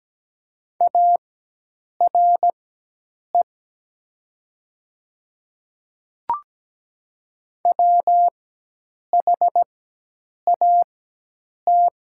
Morse code, heard as a keyed tone, about 17 wpm, spaced as for 8 wpm, 700 Hz. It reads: ARE WHAT
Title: 4x Character Spacing - Top 100 Words - 17wpm